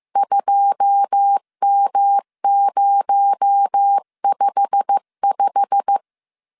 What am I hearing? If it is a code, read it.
Morse code, heard as a keyed tone, about 15 wpm, 800 Hz. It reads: 2M055
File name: CW / DET / 1 9 1500C S C 14wpm 3245